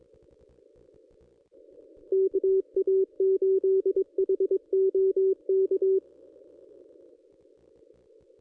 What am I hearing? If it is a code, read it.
Morse code, heard as a keyed tone, about 22 wpm, 380 Hz. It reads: KA8HOK
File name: contests / ss-cw-k5tr-op-n5rz -1400